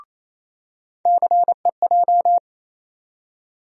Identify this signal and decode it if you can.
Morse code, heard as a keyed tone, about 28 wpm, 700 Hz. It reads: CEJ
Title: USA Call Sign Postfixes - 3 Letters 28wpm